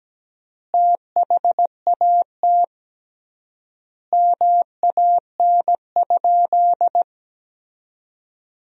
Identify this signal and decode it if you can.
Morse code, heard as a keyed tone, about 17 wpm, 700 Hz. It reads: THAT MAN?
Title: 3x Word-Spacing - Sentences from Top 200 Words - 17wpm